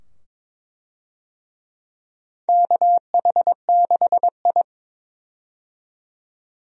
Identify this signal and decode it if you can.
Morse code, heard as a keyed tone, about 22 wpm, 700 Hz. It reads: KH6I
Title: DXCC-100 Call Signs - 4 Characters - 22wpm